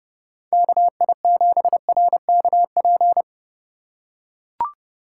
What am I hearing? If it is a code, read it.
Morse code, heard as a keyed tone, about 30 wpm, 700 Hz. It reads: KI7RKP